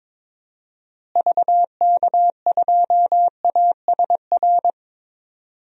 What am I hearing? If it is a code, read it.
Morse code, heard as a keyed tone, about 22 wpm, 700 Hz. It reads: VK2ASR
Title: DXCC-100 Call Signs - 6 Characters - Repeated 2x - 22wpm